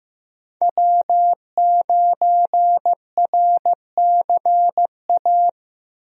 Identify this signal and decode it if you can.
Morse code, heard as a keyed tone, about 15 wpm, 700 Hz. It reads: W9RCA
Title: USA Call Signs - 5 Characters - Repeated 2x - 15wpm